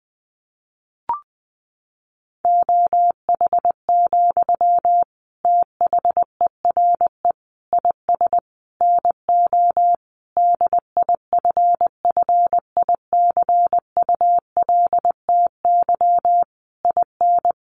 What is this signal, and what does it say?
Morse code, heard as a keyed tone, about 19 wpm, 700 Hz. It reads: OH, THERE IS NO DIFFICULTY IN
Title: Sentences from Top 1500 Words 20wpm